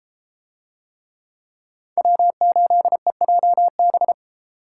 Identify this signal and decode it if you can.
Morse code, heard as a keyed tone, about 32 wpm, 700 Hz. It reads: W8EJB